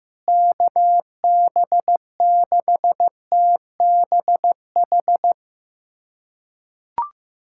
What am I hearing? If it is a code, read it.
Morse code, heard as a keyed tone, about 15 wpm, 700 Hz. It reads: KB6TBH
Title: USA Callsign DE Callsign - 6 Characters - 15wpm